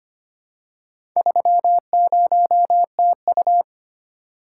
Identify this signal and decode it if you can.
Morse code, heard as a keyed tone, about 25 wpm, 700 Hz. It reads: 30TU
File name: practice / 25wpm / Four Letter-Number Combinations - 25wpm